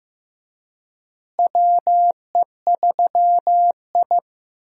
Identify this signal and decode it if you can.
Morse code, heard as a keyed tone, about 16 wpm, 700 Hz. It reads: WE3I
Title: USA Call Signs - 4 Characters - Repeated 3x - 15wpm